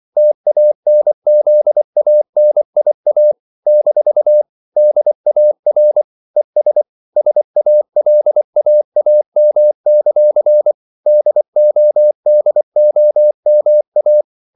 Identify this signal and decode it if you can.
Morse code, heard as a keyed tone, about 24 wpm, 600 Hz. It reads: TANZANIA - DAR ES SALAAM; DODOMA
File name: CW / Practice / NationalCapitals24